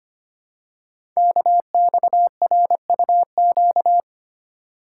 Morse code, heard as a keyed tone, about 25 wpm, 700 Hz. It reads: KXRUQ
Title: Code Groups - 5 Letters - 25wpm